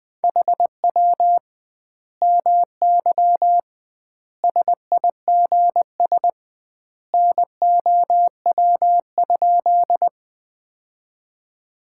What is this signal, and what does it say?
Morse code, heard as a keyed tone, about 20 wpm, 700 Hz. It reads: HW MY SIGS NOW?
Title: Common QSO Phrases - 2x Word-Spacing - 20wpm